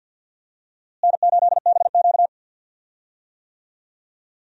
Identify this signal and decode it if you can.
Morse code, heard as a keyed tone, about 51 wpm, 700 Hz. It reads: N8BQ
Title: USA Call Signs - 4 Characters - Repeated 3x - 50wpm